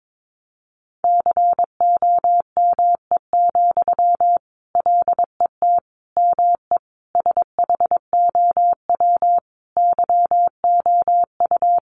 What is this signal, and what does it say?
Morse code, heard as a keyed tone, about 22 wpm, 700 Hz. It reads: COME, LET ME SHOW YOU
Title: Sentences from Top 500 Words 22wpm